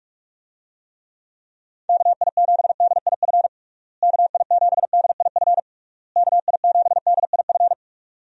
Morse code, heard as a keyed tone, about 46 wpm, 700 Hz. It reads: KI7DIF KI7DIF KI7DIF
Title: USA Call Signs - 6 Characters - Repeated 3x - 45wpm